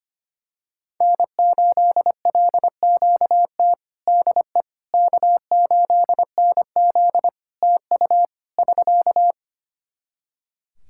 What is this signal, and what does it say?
Morse code, heard as a keyed tone, about 25 wpm, 700 Hz. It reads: N8LQT DE K8NZ TU <SK>
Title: Practice QSO - Exchange 4 - Ending - 25wpm